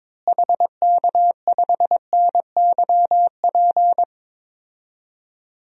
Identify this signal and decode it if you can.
Morse code, heard as a keyed tone, about 21 wpm, 700 Hz. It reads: HK5NYP
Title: DXCC-100 Call Signs - 6 Characters - Repeated 2x - 22wpm